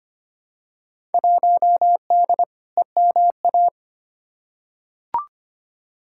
Morse code, heard as a keyed tone, about 25 wpm, 700 Hz. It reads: 1D EMA
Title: Field Day Exchanges - 25wpm